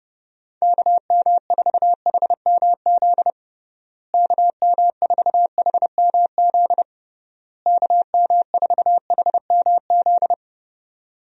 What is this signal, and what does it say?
Morse code, heard as a keyed tone, about 32 wpm, 700 Hz. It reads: KM4HMZ KM4HMZ KM4HMZ
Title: USA Call Signs - 6 Characters - Repeated 3x - 30wpm